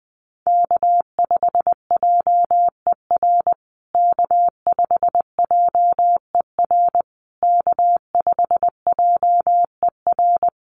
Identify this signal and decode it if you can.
Morse code, heard as a keyed tone, about 20 wpm, 700 Hz. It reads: K5JER K5JER K5JER
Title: USA Call Signs Repeated 3x 20wpm